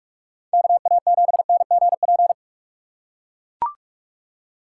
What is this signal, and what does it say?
Morse code, heard as a keyed tone, about 43 wpm, 700 Hz. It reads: KA7NGP